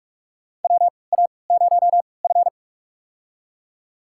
Morse code, heard as a keyed tone, about 47 wpm, spaced as for 26 wpm, 700 Hz. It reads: WA0F
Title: USA Call Signs - 3x Character Spacing - 45wpm